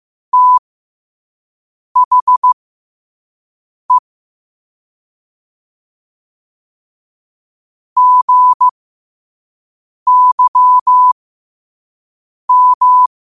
Morse code, heard as a keyed tone, about 15 wpm, spaced as for 5 wpm, 1000 Hz. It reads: THE GYM
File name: qotdfiles / 5wpm